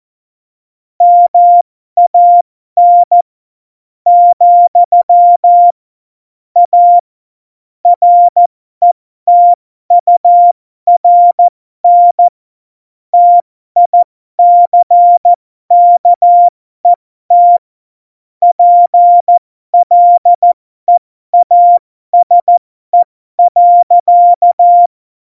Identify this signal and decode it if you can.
Morse code, heard as a keyed tone, about 14 wpm, spaced as for 12 wpm, 700 Hz. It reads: MAN , A RETURN TICKET PLEASE.
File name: Morsekurs / Witze